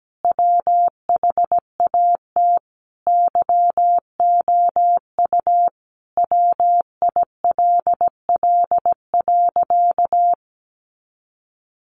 Morse code, heard as a keyed tone, about 17 wpm, 700 Hz. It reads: WHAT YOU WILL.
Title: Sentences from Top 700 Words 17wpm